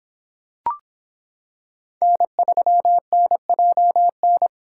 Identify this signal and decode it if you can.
Morse code, heard as a keyed tone, about 26 wpm, 700 Hz. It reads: N3NJN